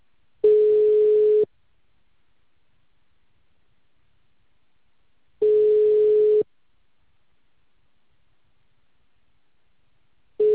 Ringback tone (Europe, 425 Hz).
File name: out- -2004- - - .7701